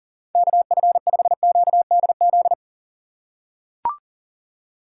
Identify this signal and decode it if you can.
Morse code, heard as a keyed tone, about 38 wpm, 700 Hz. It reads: KF5QDZ
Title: USA Call Signs - 6 Characters - 2x Character Spacing - 40wpm